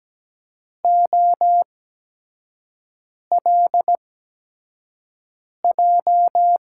Morse code, heard as a keyed tone, about 17 wpm, spaced as for 5 wpm, 700 Hz. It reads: OLJ
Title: USA Call Signs - 6 Characters - 8x Character Spacing - 17wpm